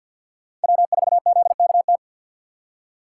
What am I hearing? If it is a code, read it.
Morse code, heard as a keyed tone, about 53 wpm, 700 Hz. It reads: W4ZKT
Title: USA Call Signs - 5 Characters - 3x Character Spacing - 50wpm